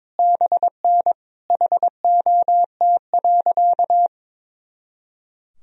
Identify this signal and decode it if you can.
Morse code, heard as a keyed tone, about 22 wpm, 700 Hz. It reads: BN HOT.
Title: Sentences from Top 500 Words and Abbreviations 22wpm